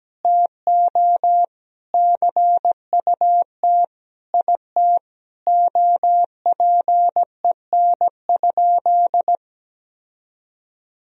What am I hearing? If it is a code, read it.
Morse code, heard as a keyed tone, about 17 wpm, 700 Hz. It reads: TO CUT IT OPEN?
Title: Sentences from Top 1000 Words - Review Part 1 - 17wpm